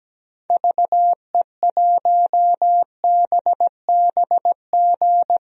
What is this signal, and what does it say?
Morse code, heard as a keyed tone, about 17 wpm, 700 Hz. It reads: VE1BBG
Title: DXCC-100 Call Signs - 6 Characters - Repeated 3x - 17wpm